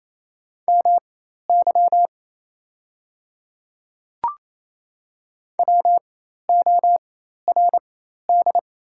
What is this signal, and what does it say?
Morse code, heard as a keyed tone, about 28 wpm, spaced as for 13 wpm, 700 Hz. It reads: MY WORD
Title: 4x Character Spacing - Top 100 Words - 28wpm